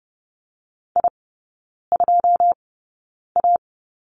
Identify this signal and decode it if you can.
Morse code, heard as a keyed tone, about 30 wpm, spaced as for 9 wpm, 700 Hz. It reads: I2A
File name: 081 Lesson 14e - Number 2 - Three Characters 30wpm